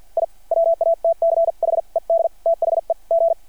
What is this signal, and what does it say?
Morse code, heard as a keyed tone, about 52 wpm, spaced as for 43 wpm, 660 Hz. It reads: I WATCHED THE G
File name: archived / CRSnet